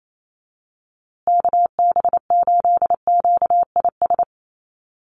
Morse code, heard as a keyed tone, about 28 wpm, 700 Hz. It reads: KB8QIS
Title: USA Call Signs Repeated 3x 28wpm